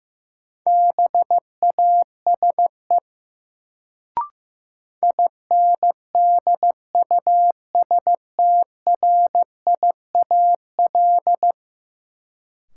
Morse code, heard as a keyed tone, about 16 wpm, 700 Hz. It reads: BASE INDUSTRIAL
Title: Sets of 5 Words - New Word Drill - 15wpm